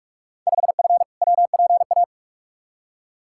Morse code, heard as a keyed tone, about 46 wpm, 700 Hz. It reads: 5F WPA
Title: Field Day Exchanges Encore - 45wpm